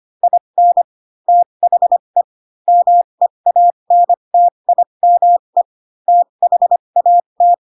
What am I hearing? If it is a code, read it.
Morse code, heard as a keyed tone, about 25 wpm, spaced as for 22 wpm, 700 Hz. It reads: IN THE MEANTIME THAT
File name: Pinocchio / Pinocchio ch4 18 25 ik1hge